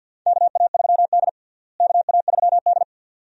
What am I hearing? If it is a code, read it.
Morse code, heard as a keyed tone, about 50 wpm, 700 Hz. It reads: KA3D KA3D
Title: USA Call Signs - 4 Characters - Repeated 2x - 50wpm